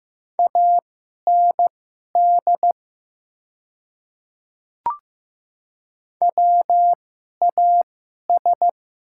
Morse code, heard as a keyed tone, about 15 wpm, spaced as for 11 wpm, 700 Hz. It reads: AND WAS
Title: 2x Character Spacing - Top 100 Words - 15wpm